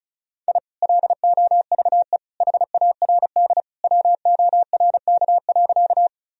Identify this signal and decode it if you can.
Morse code, heard as a keyed tone, about 36 wpm, 700 Hz. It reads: I LOVE HARD WORK.
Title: Sentences from Top 1000 Words - Review Part 1 - 35wpm